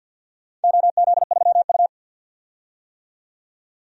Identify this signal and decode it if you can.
Morse code, heard as a keyed tone, about 48 wpm, 700 Hz. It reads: OZ3U
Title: DXCC-100 Call Signs - 4 Characters - 50wpm